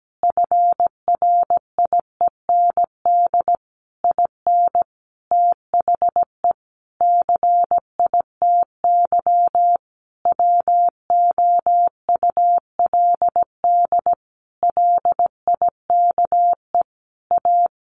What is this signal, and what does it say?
Morse code, heard as a keyed tone, about 17 wpm, 700 Hz. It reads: FRIEND IN THE CITY WOULD LIKE A
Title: Sentences from Top 1100 Words 17wpm